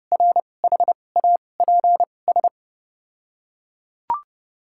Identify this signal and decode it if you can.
Morse code, heard as a keyed tone, about 31 wpm, spaced as for 22 wpm, 700 Hz. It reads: RHAPS
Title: 2x Character Spacing - Top 500 Words - 30wpm